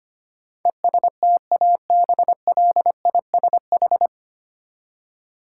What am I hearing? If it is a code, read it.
Morse code, heard as a keyed tone, about 25 wpm, 700 Hz. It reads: ESTABLISH
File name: Nine Letter Words - 25wpm